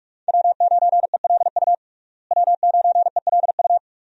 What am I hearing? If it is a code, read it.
Morse code, heard as a keyed tone, about 46 wpm, 700 Hz. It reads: W9ELU W9ELU
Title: USA Call Signs - 5 Characters - Repeated 2x - 45wpm